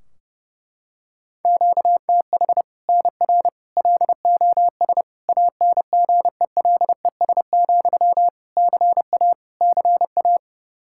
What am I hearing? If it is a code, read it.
Morse code, heard as a keyed tone, about 30 wpm, 700 Hz. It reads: QTH NR LOS ANGELES, CA CA
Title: Practice QSO - Exchange 1 - Beginning - 30wpm